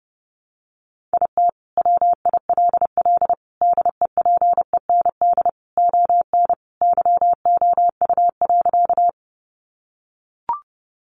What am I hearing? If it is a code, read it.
Morse code, heard as a keyed tone, about 28 wpm, 700 Hz. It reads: IT WILL DEPEND ON YOU.
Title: Sentences from Top 1300 Words 30wpm